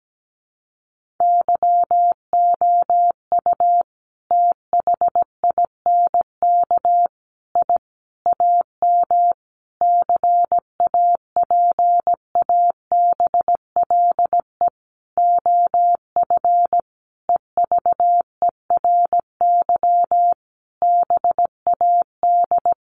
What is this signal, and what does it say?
Morse code, heard as a keyed tone, about 17 wpm, 700 Hz. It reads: YOU THINK I AM CAPABLE OF EVERY BAD
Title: Sentences from Top 1200 Words 17wpm